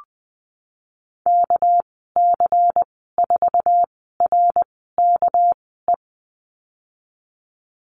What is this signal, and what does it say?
Morse code, heard as a keyed tone, about 20 wpm, spaced as for 15 wpm, 700 Hz. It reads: KC4RKE
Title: 207 Lesson 33g - Letter j - Call Signs 20wpm